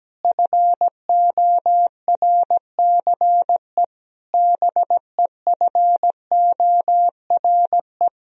Morse code, heard as a keyed tone, about 17 wpm, 700 Hz. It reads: FORCE BEFORE
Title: Sentences from Top 1000 Words - Review Part 3 - 17wpm